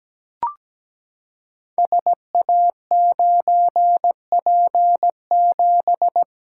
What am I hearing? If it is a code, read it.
Morse code, heard as a keyed tone, about 17 wpm, 700 Hz. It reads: SA9P7
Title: Code Groups - 5 Letter:Numbers - 17wpm